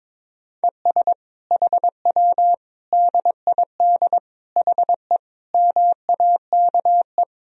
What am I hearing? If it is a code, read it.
Morse code, heard as a keyed tone, about 22 wpm, 700 Hz. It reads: ES HW DID HE MAKE
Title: Sentences from Top 400 Words and Abbreviations 22wpm